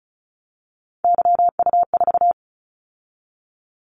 Morse code, heard as a keyed tone, about 35 wpm, 700 Hz. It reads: YU4
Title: Callsign Prefixes for Top 25 DXCC - 35wpm